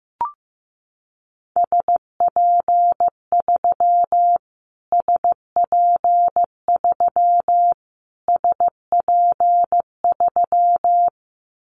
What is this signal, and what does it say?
Morse code, heard as a keyed tone, about 15 wpm, 700 Hz. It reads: SP3 SP3 SP3
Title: Callsign Prefixes for Top 25 DXCC - Repeated 3x - 15wpm